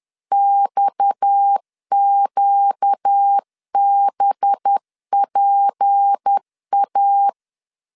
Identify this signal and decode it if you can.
Morse code, heard as a keyed tone, about 10 wpm, 800 Hz. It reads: XQBPA
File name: LEZIONI / 8 YJ45 / 8 YJ 10wpm 2651